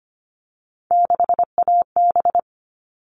Morse code, heard as a keyed tone, about 25 wpm, 700 Hz. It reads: 6AB
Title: All Cut Numbers - 3-Digit 25wpm